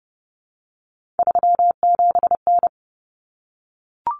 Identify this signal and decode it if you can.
Morse code, heard as a keyed tone, about 30 wpm, 700 Hz. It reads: 37N
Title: Common Cut Numbers - 3-Digit 30wpm